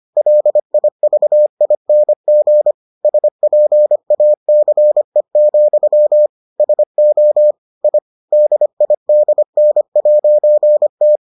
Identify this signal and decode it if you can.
Morse code, heard as a keyed tone, about 25 wpm, 600 Hz. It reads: LIVING SPACE, SO I DIDN'T